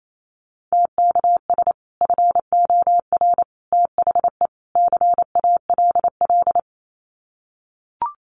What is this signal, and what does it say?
Morse code, heard as a keyed tone, about 28 wpm, 700 Hz. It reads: TKS FOR THE CALL
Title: Common QSO Phrases 28wpm